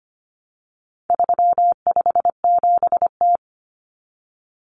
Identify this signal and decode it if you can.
Morse code, heard as a keyed tone, about 25 wpm, 700 Hz. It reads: 357T
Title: Common Cut Numbers - 4-Digit 25wpm